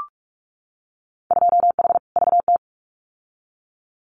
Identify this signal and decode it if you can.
Morse code, heard as a keyed tone, about 44 wpm, 700 Hz. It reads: 2H VT
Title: Winter Field Day Exchanges 45wpm